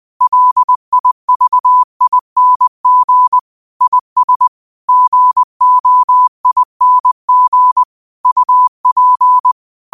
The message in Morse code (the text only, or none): LIVING IS GOING UP